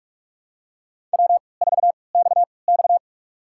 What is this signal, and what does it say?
Morse code, heard as a keyed tone, about 44 wpm, spaced as for 25 wpm, 700 Hz. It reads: W4XX
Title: USA Call Signs - 3x Character Spacing - 45wpm